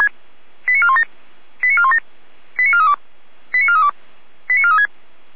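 Five-tone selective call (ZVEI), heard as quick runs of five tones, tones of about 70 ms. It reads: 89417 89417 89432 89432 89436
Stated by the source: Five-tone ZVEI selective call received by SDR.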